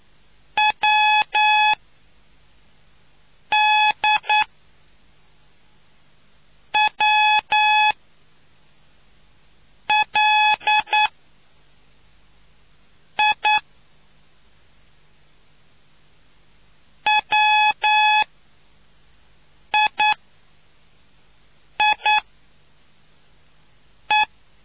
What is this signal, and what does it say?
Morse code, heard as a keyed tone, about 9 wpm, spaced as for 4 wpm, 850 Hz. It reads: W D W L I WI I E